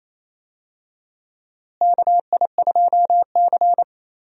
Morse code, heard as a keyed tone, about 28 wpm, 700 Hz. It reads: KI2C
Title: USA Call Signs - 4 Characters - Repeated 2x - 28wpm